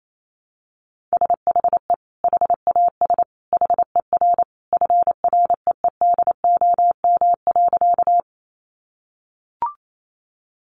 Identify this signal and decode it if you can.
Morse code, heard as a keyed tone, about 28 wpm, 700 Hz. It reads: SHE HAS HER FREEDOM.
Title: Sentences from Top 800 Words 28wpm